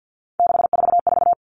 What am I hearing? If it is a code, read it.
Morse code, heard as a keyed tone, about 50 wpm, 700 Hz. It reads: 644
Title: All Cut Numbers - 3-Digit 50wpm